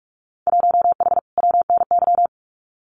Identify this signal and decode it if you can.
Morse code, heard as a keyed tone, about 44 wpm, 700 Hz. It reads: 1H WNY
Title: Winter Field Day Exchanges 45wpm